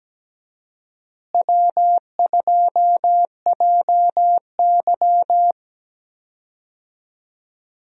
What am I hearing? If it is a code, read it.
Morse code, heard as a keyed tone, about 17 wpm, 700 Hz. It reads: W2JY
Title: USA Call Signs - 4 Characters - Repeated 3x - 17wpm